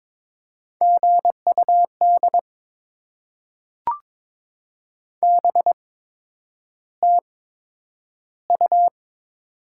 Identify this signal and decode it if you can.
Morse code, heard as a keyed tone, about 22 wpm, spaced as for 9 wpm, 700 Hz. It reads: GUD BTU
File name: Top 20 QSO Elements - 8x Character Spacing - 22wpm